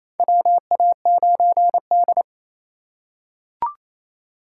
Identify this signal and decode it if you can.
Morse code, heard as a keyed tone, about 28 wpm, 700 Hz. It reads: WA9D
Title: USA Call Signs - 4 Characters - 4x Character Spacing - 28wpm